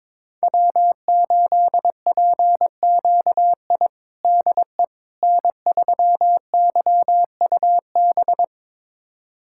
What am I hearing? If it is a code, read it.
Morse code, heard as a keyed tone, about 22 wpm, 700 Hz. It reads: W8PQI DE N3YUB
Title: USA Callsign DE Callsign - 5 Characters - 22wpm